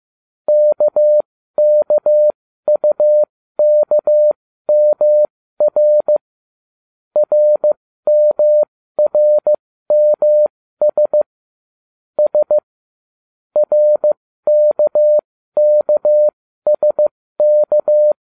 Morse code, heard as a keyed tone, about 15 wpm, spaced as for 13 wpm, 600 Hz. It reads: KKUKMR RMRMS S RKKSK